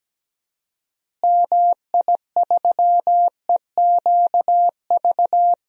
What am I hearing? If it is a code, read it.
Morse code, heard as a keyed tone, about 17 wpm, 700 Hz. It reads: MI3EQV
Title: DXCC-100 Call Signs - 6 Characters - Repeated 3x - 17wpm